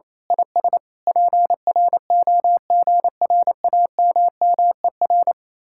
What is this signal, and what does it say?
Morse code, heard as a keyed tone, about 26 wpm, 700 Hz. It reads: IS PROGRAMMER
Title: Speed-Racing Common QSO Phrases Encore - 28wpm